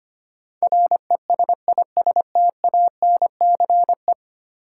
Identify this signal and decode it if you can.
Morse code, heard as a keyed tone, about 26 wpm, 700 Hz. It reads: RESISTANCE
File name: 10 Letter Words - First 3 Letters Given - 25wpm